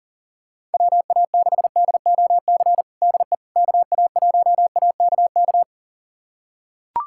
Morse code, heard as a keyed tone, about 40 wpm, 700 Hz. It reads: WA6DOC DE KA1AKK
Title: USA Callsign DE Callsign - 6 Characters - 40wpm